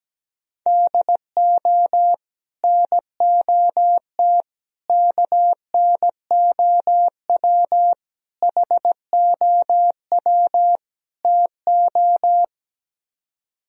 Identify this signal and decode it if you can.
Morse code, heard as a keyed tone, about 17 wpm, 700 Hz. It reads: DO NOT KNOW HOW TO
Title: Speed-Racing - Sets of 5 Words 17wpm